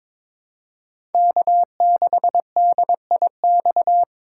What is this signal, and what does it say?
Morse code, heard as a keyed tone, about 22 wpm, 700 Hz. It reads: K6DIX
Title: USA Call Signs - 3x Character Spacing - 22wpm